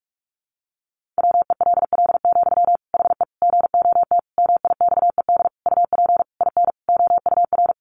Morse code, heard as a keyed tone, about 44 wpm, 700 Hz. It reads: WELL, HE GOT MIXED UP IN OUR
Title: Sentences from Top 1500 Words 45wpm